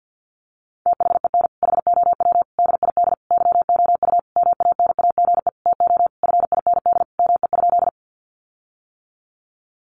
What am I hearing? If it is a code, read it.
Morse code, heard as a keyed tone, about 50 wpm, 700 Hz. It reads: THEN HOW DID YOU MANAGE TO FIND ME?
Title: Sentences from Top 1400 Words 50wpm